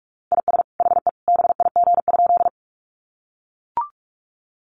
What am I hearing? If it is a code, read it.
Morse code, heard as a keyed tone, about 44 wpm, 700 Hz. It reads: IS HE BIG?
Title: Sentences from Top 200 Words 45wpm